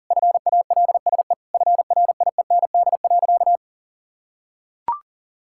Morse code, heard as a keyed tone, about 37 wpm, 700 Hz. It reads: FALSE FRIEND.
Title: Sentences from Top 1600 Words 40wpm